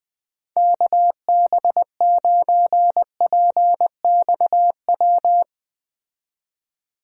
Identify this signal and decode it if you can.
Morse code, heard as a keyed tone, about 20 wpm, 700 Hz. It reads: KB9PXW
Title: Speed Racing - USA 2x3 Call Signs 17wpm